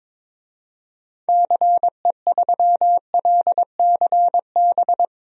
Morse code, heard as a keyed tone, about 22 wpm, 700 Hz. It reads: CE3LCB
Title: DXCC-100 Call Signs - 6 Characters - 22wpm